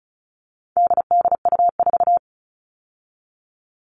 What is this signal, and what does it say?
Morse code, heard as a keyed tone, about 34 wpm, 700 Hz. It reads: DDU4
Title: All Cut Numbers - 4-Digit 35wpm